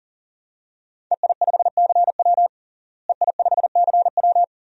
Morse code, heard as a keyed tone, about 40 wpm, 700 Hz. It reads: EI5CW EI5CW
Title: DXCC-100 Call Signs - 5 Characters - Repeated 2x - 40wpm